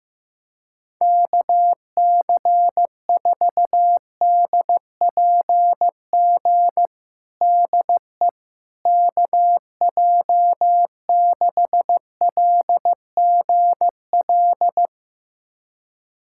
Morse code, heard as a keyed tone, about 15 wpm, 700 Hz. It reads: KC4DPG DE KJ6LGL
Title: USA Callsign DE Callsign - 6 Characters - 15wpm